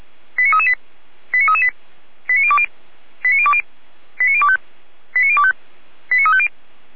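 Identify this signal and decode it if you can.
Five-tone selective call (ZVEI), heard as quick runs of five tones, tones of about 80 ms. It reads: 89338 89338 89020 89020 89025 89025 89350
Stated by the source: ZVEI five-tone selective call 89280 received by SDR.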